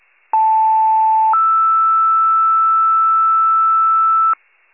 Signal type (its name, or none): two-tone page